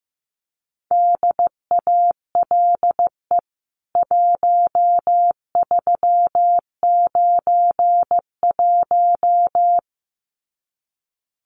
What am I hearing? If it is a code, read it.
Morse code, heard as a keyed tone, about 15 wpm, 700 Hz. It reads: DALE 1391
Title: CWT Exchange 15wpm